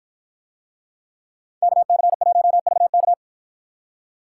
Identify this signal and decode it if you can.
Morse code, heard as a keyed tone, about 53 wpm, 700 Hz. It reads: KC1VK